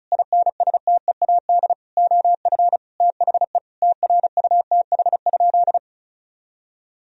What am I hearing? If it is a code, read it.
Morse code, heard as a keyed tone, about 34 wpm, 700 Hz. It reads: INSTEAD OF THE TRUTH?
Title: Sentences from Top 1000 Words - Review Part 4 - 35wpm